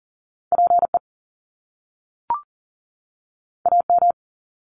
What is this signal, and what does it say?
Morse code, heard as a keyed tone, about 42 wpm, 700 Hz. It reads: PE AM
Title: Top Two Letter Combinations in Words 40wpm